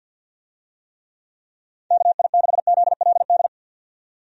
Morse code, heard as a keyed tone, about 50 wpm, 700 Hz. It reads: KI6ZLD
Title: USA Call Signs - 6 Characters - 50wpm